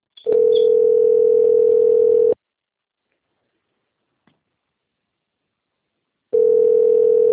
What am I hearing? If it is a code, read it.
Ringback tone (North America), 440 + 480 Hz.